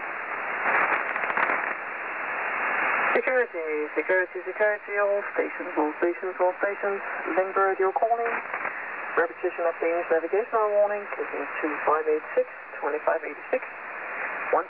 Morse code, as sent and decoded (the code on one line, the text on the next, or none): none